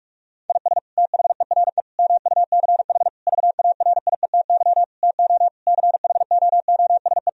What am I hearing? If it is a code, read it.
Morse code, heard as a keyed tone, about 41 wpm, 700 Hz. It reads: IS THERE MUCH VARIETY TO CHOOSE
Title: Sentences from Top 1000 Words - Review Part 4 - 45wpm